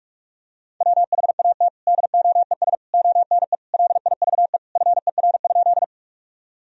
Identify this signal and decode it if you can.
Morse code, heard as a keyed tone, about 42 wpm, 700 Hz. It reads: WHAT DOES ONE LIVE FER?
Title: Sentences from Top 200 Words and Abbreviations 45wpm